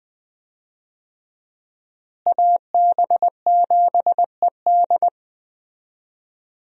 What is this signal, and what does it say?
Morse code, heard as a keyed tone, about 21 wpm, 700 Hz. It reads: AB7ED
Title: USA Call Signs - 5 Characters - Repeated 3x - 20wpm